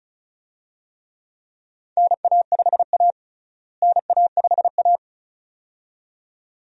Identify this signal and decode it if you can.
Morse code, heard as a keyed tone, about 36 wpm, 700 Hz. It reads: NA5A NA5A
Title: USA Call Signs - 4 Characters - Repeated 2x - 35wpm